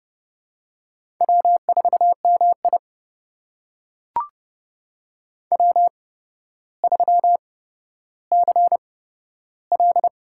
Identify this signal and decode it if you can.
Morse code, heard as a keyed tone, about 28 wpm, spaced as for 13 wpm, 700 Hz. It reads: W4MI W3CL
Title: USA Call Signs - 4 Characters - 8x Character Spacing - 30wpm